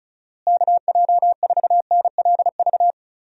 Morse code, heard as a keyed tone, about 34 wpm, 700 Hz. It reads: KJ4NLV
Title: USA Call Signs - 6 Characters - Repeated 2x - 35wpm